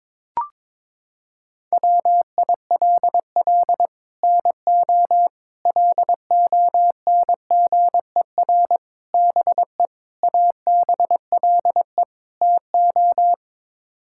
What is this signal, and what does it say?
Morse code, heard as a keyed tone, about 22 wpm, 700 Hz. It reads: WILL NO LONGER BE ABLE TO
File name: practice / 22wpm / Sets of 6 Words - 22wpm